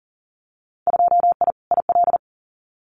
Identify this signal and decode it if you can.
Morse code, heard as a keyed tone, about 40 wpm, 700 Hz. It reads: 2I IL
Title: Winter Field Day Exchanges Encore 40wpm